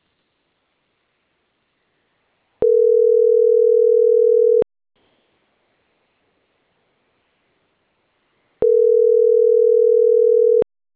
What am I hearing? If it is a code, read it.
Ringback tone (North America), 440 + 480 Hz.